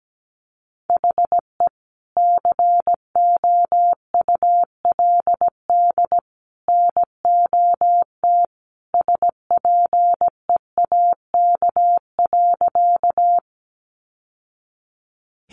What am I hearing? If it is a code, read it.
Morse code, heard as a keyed tone, about 17 wpm, 700 Hz. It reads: HE COULD NOT SPEAK.
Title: Sentences from Top 1100 Words 17wpm